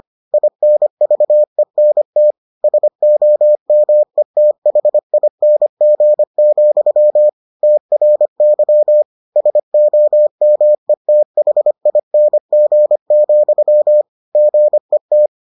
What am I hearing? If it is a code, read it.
Morse code, heard as a keyed tone, about 25 wpm, 600 Hz. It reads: INVENT SOMETHING, TRY SOMETHING, GET